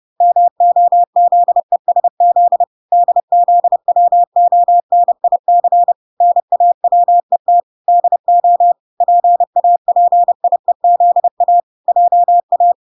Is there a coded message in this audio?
Yes: Morse code from 0.2 to 12.7 s, heard as a keyed tone, about 30 wpm, 700 Hz. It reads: MOZESZ DZWONIC NAWET DO PAPIEZA JA